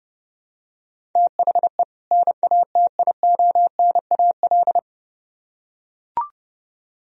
Morse code, heard as a keyed tone, about 29 wpm, 700 Hz. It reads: THE NATIONAL